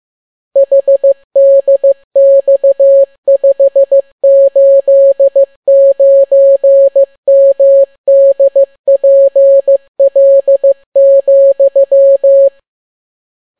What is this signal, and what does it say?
Morse code, heard as a keyed tone, about 15 wpm, 550 Hz. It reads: HDX589MDPL,